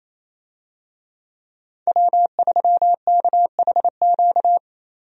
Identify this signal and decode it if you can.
Morse code, heard as a keyed tone, about 28 wpm, 700 Hz. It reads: W3KHQ
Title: USA Call Signs - 5 Characters - Repeated 2x - 28wpm